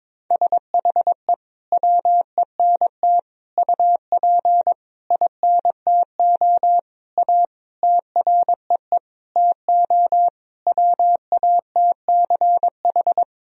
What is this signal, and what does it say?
Morse code, heard as a keyed tone, about 22 wpm, 700 Hz. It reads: SHE WENT UP INTO A TREE TO WATCH
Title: Sentences from Top 1000 Words - Review Part 1 - 22wpm